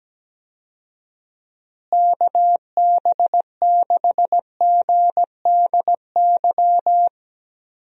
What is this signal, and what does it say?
Morse code, heard as a keyed tone, about 17 wpm, 700 Hz. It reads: KB6GDY
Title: USA Call Signs - 6 Characters - Repeated 3x - 17wpm